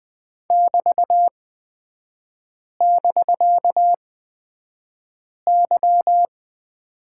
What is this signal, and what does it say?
Morse code, heard as a keyed tone, about 20 wpm, spaced as for 5 wpm, 700 Hz. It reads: <BT><BK>Y
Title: 250 Lesson 40e - BK - Three Characters 20wpm